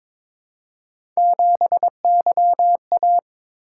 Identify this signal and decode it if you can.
Morse code, heard as a keyed tone, about 23 wpm, 700 Hz. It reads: 7YA
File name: practice / 22wpm / Three Letter-Number Combinations - 22wpm